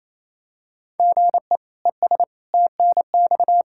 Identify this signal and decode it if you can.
Morse code, heard as a keyed tone, about 28 wpm, 700 Hz. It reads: GE ES TNX
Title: Practice QSO - Exchange 1 - Beginning - 28wpm